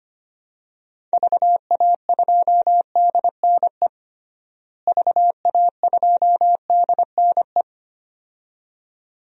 Morse code, heard as a keyed tone, about 25 wpm, 700 Hz. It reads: VA2DNE VA2DNE
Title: DXCC-100 Call Signs - 6 Characters - Repeated 2x - 25wpm